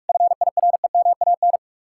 Morse code, heard as a keyed tone, about 58 wpm, 700 Hz. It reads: FIREMAN